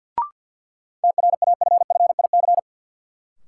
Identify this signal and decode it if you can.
Morse code, heard as a keyed tone, about 53 wpm, 700 Hz. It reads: TRAFFIC
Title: Sentences from Top 1300 Words - New Word Drill - 50wpm